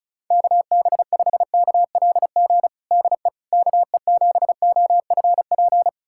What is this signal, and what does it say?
Morse code, heard as a keyed tone, about 36 wpm, 700 Hz. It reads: KB5KLG DE KE7OFP
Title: USA Callsign DE Callsign 35wpm